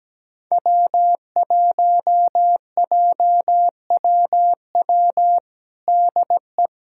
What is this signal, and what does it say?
Morse code, heard as a keyed tone, about 17 wpm, 700 Hz. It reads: W1JWW DE
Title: USA Callsign DE Callsign 17wpm